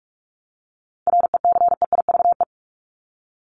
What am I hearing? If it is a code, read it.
Morse code, heard as a keyed tone, about 44 wpm, 700 Hz. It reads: RECEIVE
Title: ED Suffix - Top 100 Words with Base Words 45wpm